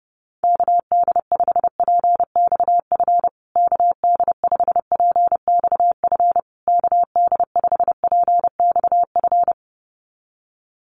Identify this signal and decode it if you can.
Morse code, heard as a keyed tone, about 30 wpm, 700 Hz. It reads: KD5PXF KD5PXF KD5PXF
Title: USA Call Signs Repeated 3x 30wpm